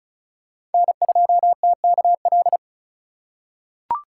Morse code, heard as a keyed tone, about 36 wpm, 700 Hz. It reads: N2TKL
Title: USA Call Signs - 5 Characters - 2x Character Spacing - 35wpm